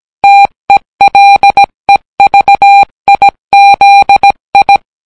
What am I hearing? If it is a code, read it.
Morse code, heard as a keyed tone, about 17 wpm, 800 Hz. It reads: TELEVIZI